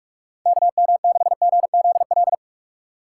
Morse code, heard as a keyed tone, about 42 wpm, 700 Hz. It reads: KM6GZL
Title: USA Call Signs - 4x Character Spacing - 45wpm